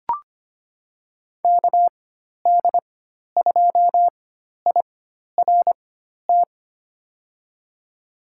Morse code, heard as a keyed tone, about 25 wpm, spaced as for 12 wpm, 700 Hz. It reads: KD2IRT
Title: USA Call Signs - 4x Character Spacing - 25wpm